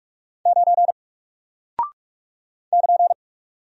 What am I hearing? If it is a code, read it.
Morse code, heard as a keyed tone, about 45 wpm, 700 Hz. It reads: 9 <KN>
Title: Single Character (Letter:Number:Punctuation:Prosign) Alt 45wpm